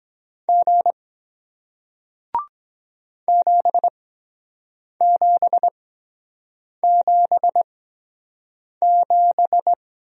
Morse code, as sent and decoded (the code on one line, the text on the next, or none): --. / --... --... --... --...
G 7777